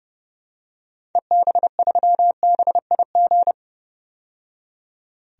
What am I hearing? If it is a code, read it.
Morse code, heard as a keyed tone, about 30 wpm, 700 Hz. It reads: EB3BIG
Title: DXCC-100 Call Signs - 6 Characters - 30wpm